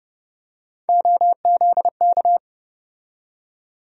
Morse code, heard as a keyed tone, about 32 wpm, 700 Hz. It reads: OZK
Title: USA Call Sign Postfixes - 3 Letters 30wpm